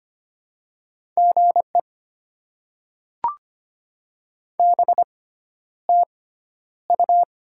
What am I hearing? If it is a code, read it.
Morse code, heard as a keyed tone, about 25 wpm, spaced as for 11 wpm, 700 Hz. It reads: GE BTU
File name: Top 20 QSO Elements - 6x Character Spacing - 25wpm